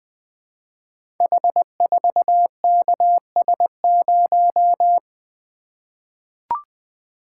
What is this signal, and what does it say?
Morse code, heard as a keyed tone, about 20 wpm, 700 Hz. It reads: H4KS0
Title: Code Groups - 5 Letter:Numbers - 20wpm